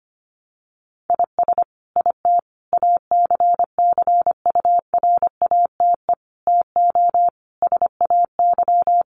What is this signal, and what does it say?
Morse code, heard as a keyed tone, about 25 wpm, 700 Hz. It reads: IS IT ACCURATE TO SAY
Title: Sentences from Top 1200 Words 25wpm